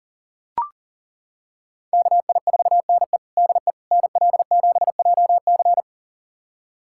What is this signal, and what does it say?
Morse code, heard as a keyed tone, about 40 wpm, 700 Hz. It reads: KI4NE DE NL7JC
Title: USA Callsign DE Callsign 40wpm